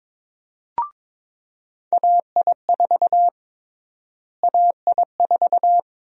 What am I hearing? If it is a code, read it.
Morse code, heard as a keyed tone, about 22 wpm, 700 Hz. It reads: AI4 AI4
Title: USA Call Sign Prefixes - Repeated 2x - 22wpm